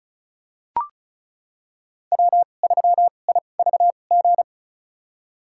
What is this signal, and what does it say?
Morse code, heard as a keyed tone, about 36 wpm, spaced as for 25 wpm, 700 Hz. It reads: W3IVG
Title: USA Call Signs - 2x Character Spacing - 35wpm